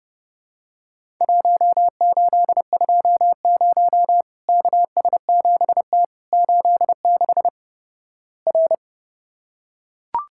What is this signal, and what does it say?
Morse code, heard as a keyed tone, about 30 wpm, 700 Hz. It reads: 1820 KS7T 86 R
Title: Worked All Europe DX Contest - QTC Exchange - 30wpm